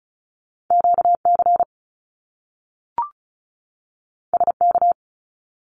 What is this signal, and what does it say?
Morse code, heard as a keyed tone, about 36 wpm, 700 Hz. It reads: QC SK
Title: Canadian Provinces and Territories 35wpm